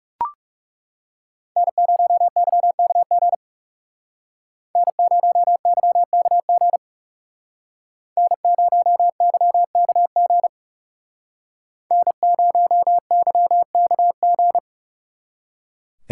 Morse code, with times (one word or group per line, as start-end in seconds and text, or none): 1.6-3.4 N0YKG
4.8-6.8 N0YKG
8.2-10.5 N0YKG
11.9-14.6 N0YKG